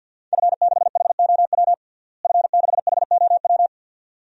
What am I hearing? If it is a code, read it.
Morse code, heard as a keyed tone, about 47 wpm, 700 Hz. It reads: F6HOW F6HOW
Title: DXCC-100 Call Signs - 5 Characters - Repeated 2x - 50wpm